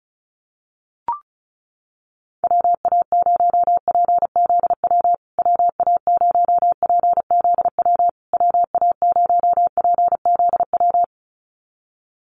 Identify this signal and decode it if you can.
Morse code, heard as a keyed tone, about 34 wpm, 700 Hz. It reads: WA0PZW WA0PZW WA0PZW
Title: USA Call Signs Repeated 3x 35wpm